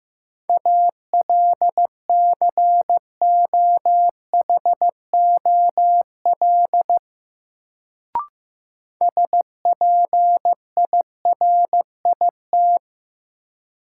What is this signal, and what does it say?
Morse code, heard as a keyed tone, about 15 wpm, 700 Hz. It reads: ALCOHOL SPIRIT
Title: Sentences from Top 1400 Words - New Word Drill - 15wpm